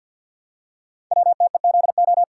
Morse code, heard as a keyed tone, about 50 wpm, 700 Hz. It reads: WTEZO
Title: Code Groups - 5 Letters - 50wpm